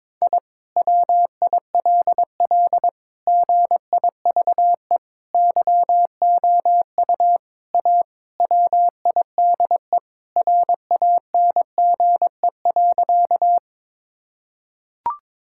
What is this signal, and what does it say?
Morse code, heard as a keyed tone, about 22 wpm, 700 Hz. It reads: I WILL GIVE YOU A WIDE RANGE.
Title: Sentences from Top 1000 Words - Review Part 6 - 22wpm